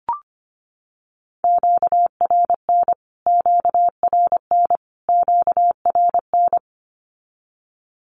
Morse code, heard as a keyed tone, about 25 wpm, 700 Hz. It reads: QRN QRN QRN
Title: Q-Codes Repeated 3x 25wpm